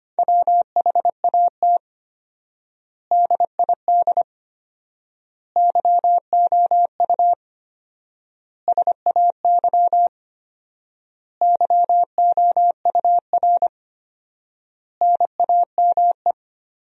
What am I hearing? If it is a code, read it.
Morse code, heard as a keyed tone, about 25 wpm, 700 Hz. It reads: WHAT DID YOU SAY YOUR NAME
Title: 4x Word-Spacing - Sentences from Top 200 Words - 25wpm